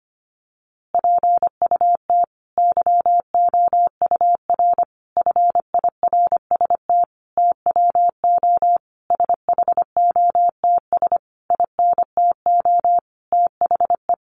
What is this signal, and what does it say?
Morse code, heard as a keyed tone, about 25 wpm, 700 Hz. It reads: PUT YOUR FIRST TWO SHOTS INTO THE
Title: A Sound Of Thunder 25wpm